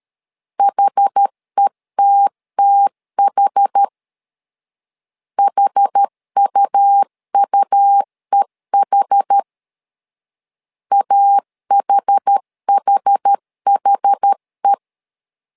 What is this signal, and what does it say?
Morse code, heard as a keyed tone, about 13 wpm, 800 Hz. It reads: HETTH HUUEH AHHHE
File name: LEZIONI / 2 UHDN / 2 UH 12wpm 1439